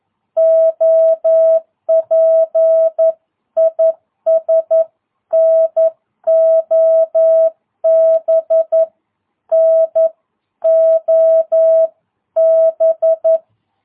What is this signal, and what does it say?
Morse code, heard as a keyed tone, about 11 wpm, 650 Hz. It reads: OP IS NOB NOB
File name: jh8cbh / 2022cw